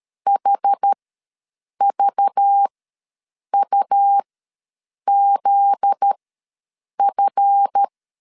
Morse code, heard as a keyed tone, about 13 wpm, spaced as for 7 wpm, 800 Hz. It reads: HVUZF
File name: LEZIONI / 7 GF9Z / 7 GF9Z ws 12wpm 3321